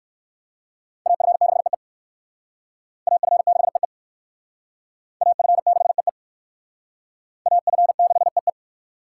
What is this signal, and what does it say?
Morse code, heard as a keyed tone, about 56 wpm, 700 Hz. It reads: AF6EE AF6EE AF6EE AF6EE
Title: Speed Racing - USA 2x2 Call Signs 45wpm